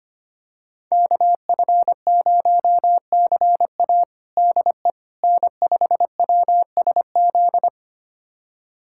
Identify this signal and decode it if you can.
Morse code, heard as a keyed tone, about 25 wpm, 700 Hz. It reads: KF0CA DE N5WSZ
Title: USA Callsign DE Callsign - 5 Characters - 25wpm